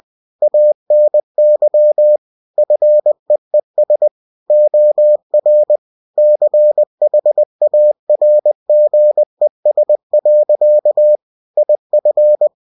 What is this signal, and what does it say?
Morse code, heard as a keyed tone, about 20 wpm, 600 Hz. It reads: ANY FEES OR CHARGES. IF
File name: winnie-20wpm